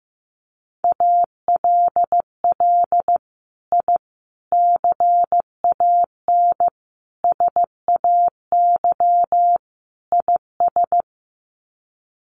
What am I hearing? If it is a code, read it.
Morse code, heard as a keyed tone, about 15 wpm, 700 Hz. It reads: ALL I CAN SAY IS
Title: Sets of 5 Words Encore2 15wpm